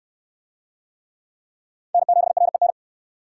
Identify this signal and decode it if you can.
Morse code, heard as a keyed tone, about 62 wpm, 700 Hz. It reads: N7REN